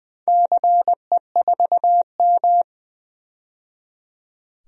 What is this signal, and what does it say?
Morse code, heard as a keyed tone, about 21 wpm, 700 Hz. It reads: CE4M
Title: DXCC-100 Call Signs - 4 Characters - 20wpm